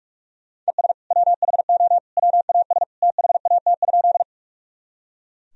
Morse code, heard as a keyed tone, about 46 wpm, 700 Hz. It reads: ES WHO WAS THAT?
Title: Sentences from Top 100 Words and Abbreviations 45wpm